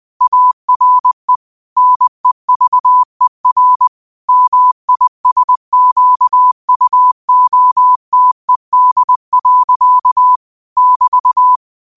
Morse code, heard as a keyed tone, about 20 wpm, 1000 Hz. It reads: ARE NEVER MISQUOTED. <BT>
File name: qotdfiles / 20wpm